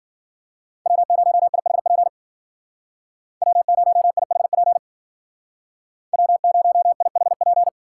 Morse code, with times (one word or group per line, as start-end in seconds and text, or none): 0.9-2.1 W0IHP
3.4-4.8 W0IHP
6.1-7.7 W0IHP